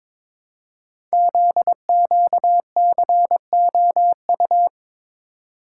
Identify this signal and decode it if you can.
Morse code, heard as a keyed tone, about 22 wpm, 700 Hz. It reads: ZQCOU
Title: Code Groups - 5 Letters - 22wpm